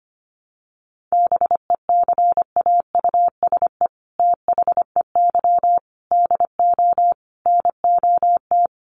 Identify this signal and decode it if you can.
Morse code, heard as a keyed tone, about 25 wpm, 700 Hz. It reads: BECAUSE THEY DO NOT
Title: Sets of 5 Words Encore2 25wpm